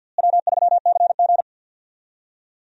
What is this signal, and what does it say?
Morse code, heard as a keyed tone, about 52 wpm, 700 Hz. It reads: W3CG
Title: USA Call Signs - 4 Characters - Repeated 3x - 50wpm